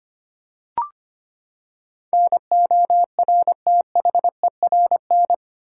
Morse code, heard as a keyed tone, about 25 wpm, 700 Hz. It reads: NORTHERN
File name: Eight Letter Words - 25wpm